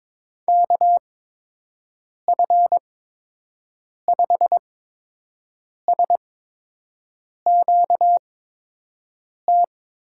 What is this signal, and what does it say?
Morse code, heard as a keyed tone, about 22 wpm, spaced as for 6 wpm, 700 Hz. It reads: KF5SQT
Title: USA Call Signs - 6 Characters - 8x Character Spacing - 22wpm